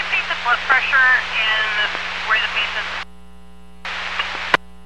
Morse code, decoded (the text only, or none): none